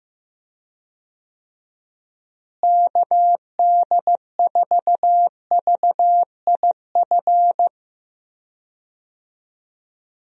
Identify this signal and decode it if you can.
Morse code, heard as a keyed tone, about 15 wpm, 700 Hz. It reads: KD4VIF
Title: USA Call Signs - 6 Characters - Repeated 3x - 15wpm